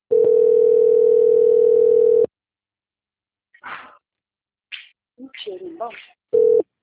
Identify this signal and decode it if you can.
Ringback tone (North America), 440 + 480 Hz, with speech.
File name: out- - - - - .18269